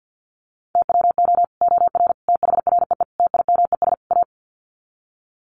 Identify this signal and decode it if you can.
Morse code, heard as a keyed tone, about 50 wpm, 700 Hz. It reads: TWO OR THREE TIMES A
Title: Sets of 5 Words Encore2 50wpm